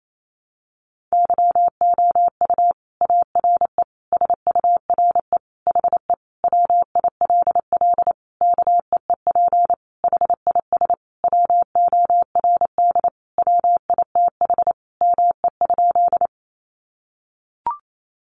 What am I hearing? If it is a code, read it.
Morse code, heard as a keyed tone, about 28 wpm, 700 Hz. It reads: YOU ARE SURE HE WILL KEEP HIS WORD WITH ME?
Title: Sentences from Top 300 Words 28wpm